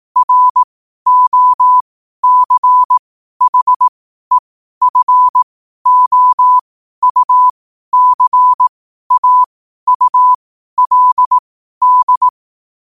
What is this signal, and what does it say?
Morse code, heard as a keyed tone, about 18 wpm, spaced as for 12 wpm, 1000 Hz. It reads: ROCHEFOUCAULD
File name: 12wpm